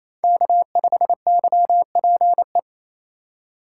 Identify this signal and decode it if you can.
Morse code, heard as a keyed tone, about 32 wpm, 700 Hz. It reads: K5YPE
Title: Speed Racing - USA 1x3 Call Signs 28wpm